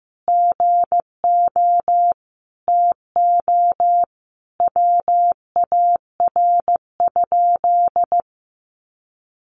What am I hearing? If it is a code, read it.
Morse code, heard as a keyed tone, about 15 wpm, 700 Hz. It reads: GO TO WAR?
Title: Sentences from Top 1400 Words 15wpm